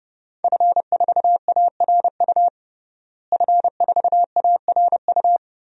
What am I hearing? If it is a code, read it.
Morse code, heard as a keyed tone, about 30 wpm, 700 Hz. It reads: F4ARU F4ARU
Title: DXCC-100 Call Signs - 5 Characters - Repeated 3x - 30wpm